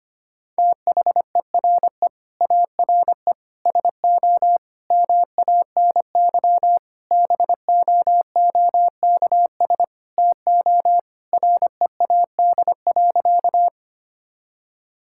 Morse code, heard as a keyed tone, about 25 wpm, 700 Hz. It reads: THERE ARE SO MANY BOOKS TO READ.
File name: practice / 25wpm / Sentences with 4-Word Phrases - 25wpm